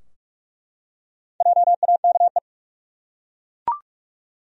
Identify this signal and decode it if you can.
Morse code, heard as a keyed tone, about 46 wpm, 700 Hz. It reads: JAKE
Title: USA Common Names - Max 5 Letters - 45wpm